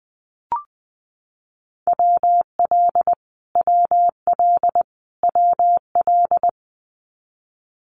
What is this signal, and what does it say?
Morse code, heard as a keyed tone, about 20 wpm, 700 Hz. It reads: WL WL WL
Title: Abbreviations Repeated 3x - 20wpm